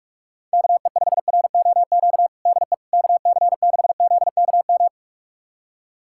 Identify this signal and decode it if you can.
Morse code, heard as a keyed tone, about 45 wpm, 700 Hz. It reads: KE5ROQ DE KC6ZKM
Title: USA Callsign DE Callsign - 6 Characters - 45wpm